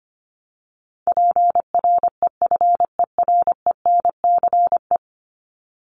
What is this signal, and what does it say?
Morse code, heard as a keyed tone, about 25 wpm, 700 Hz. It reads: PREFERENCE
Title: Top 1251-1500 Words 25wpm